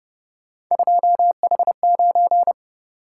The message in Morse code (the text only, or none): 2H9